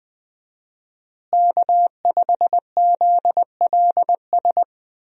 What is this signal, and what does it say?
Morse code, heard as a keyed tone, about 20 wpm, 700 Hz. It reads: K5ZLS
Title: USA Call Signs - 5 Characters - Repeated 3x - 20wpm